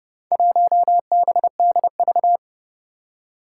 Morse code, heard as a keyed tone, about 31 wpm, 700 Hz. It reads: 1BDV
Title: Four Challenging Letter-Number Combinations 30wpm